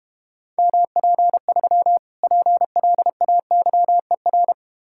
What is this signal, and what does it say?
Morse code, heard as a keyed tone, about 32 wpm, 700 Hz. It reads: MP3 PLAYER